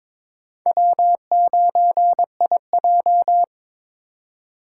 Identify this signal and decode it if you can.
Morse code, heard as a keyed tone, about 22 wpm, 700 Hz. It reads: W9IJ
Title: USA Call Signs - 2x Character Spacing - 22wpm